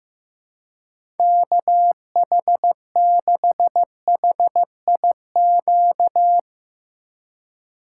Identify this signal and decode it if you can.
Morse code, heard as a keyed tone, about 15 wpm, 700 Hz. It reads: KH6HIQ
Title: USA Call Signs - 6 Characters - 4x Character Spacing - 15wpm